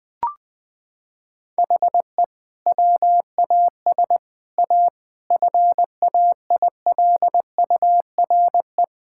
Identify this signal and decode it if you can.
Morse code, heard as a keyed tone, about 20 wpm, 700 Hz. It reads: HE WAS A FAILURE
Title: Sentences from Top 1000 Words - Review Part 6 - 20wpm